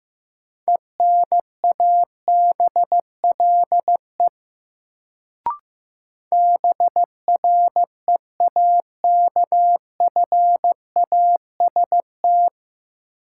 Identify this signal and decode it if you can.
Morse code, heard as a keyed tone, about 15 wpm, 700 Hz. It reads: ENABLE BREAKFAST
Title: Sentences from Top 1500 Words - New Word Drill Alt - 15wpm